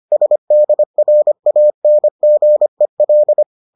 Morse code, heard as a keyed tone, about 25 wpm, 600 Hz. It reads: SDRANGEL